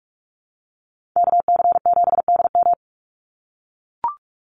Morse code, heard as a keyed tone, about 44 wpm, 700 Hz. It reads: KC7DM
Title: NA SOTA Activator Callsigns 45wpm